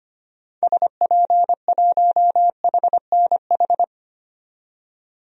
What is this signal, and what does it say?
Morse code, heard as a keyed tone, about 25 wpm, 700 Hz. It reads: SP1HNH
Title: DXCC-100 Call Signs - 6 Characters - 25wpm